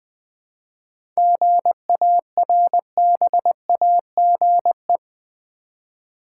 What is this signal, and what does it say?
Morse code, heard as a keyed tone, about 21 wpm, 700 Hz. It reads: GARBAGE
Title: Seven Letter Words - 20wpm